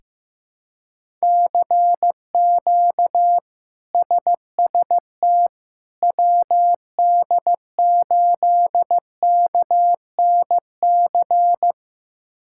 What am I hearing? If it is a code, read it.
Morse code, heard as a keyed tone, about 15 wpm, 700 Hz. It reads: CQ SST WD8KNC
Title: K1USN SST QSO Practice - 15wpm